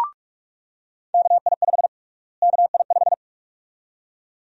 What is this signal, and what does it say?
Morse code, heard as a keyed tone, about 46 wpm, 700 Hz. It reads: KI5 KI5
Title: USA Call Sign Prefixes - Repeated 2x - 45wpm